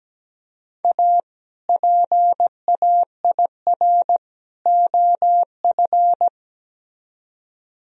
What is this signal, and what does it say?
Morse code, heard as a keyed tone, about 18 wpm, 700 Hz. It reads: A PAIR OF
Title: Speed-Racing - Sets of 3 Words 17wpm